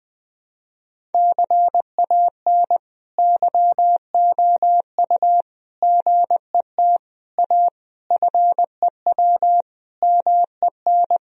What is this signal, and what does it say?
Morse code, heard as a keyed tone, about 21 wpm, 700 Hz. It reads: CAN YOU GET A FEW MEN
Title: Sentences from Top 1000 Words - Review Part 1 - 20wpm